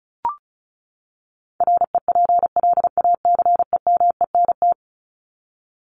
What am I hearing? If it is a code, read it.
Morse code, heard as a keyed tone, about 34 wpm, 700 Hz. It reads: REPLACEMENT
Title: Top 1751-2000 Words 35wpm